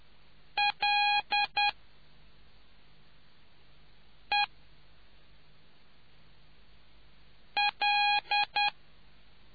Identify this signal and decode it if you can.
Morse code, heard as a keyed tone, about 10 wpm, 820 Hz. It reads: L E L